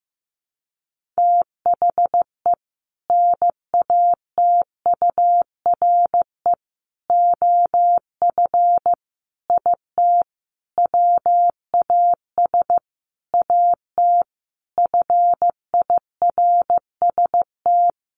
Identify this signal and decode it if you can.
Morse code, heard as a keyed tone, about 15 wpm, 700 Hz. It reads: THE NATURE OF IT WAS AT FIRST
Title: Sentences from Top 1400 Words 15wpm